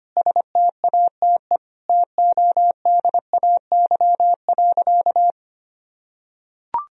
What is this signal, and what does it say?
Morse code, heard as a keyed tone, about 25 wpm, 700 Hz. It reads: STATE TODAY.
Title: Sentences from Top 1000 Words - Review Part 2 - 25wpm